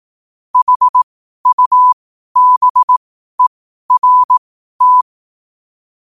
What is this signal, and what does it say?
Morse code, heard as a keyed tone, about 18 wpm, spaced as for 12 wpm, 1000 Hz. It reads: HUBERT
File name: qotdfiles / 12wpm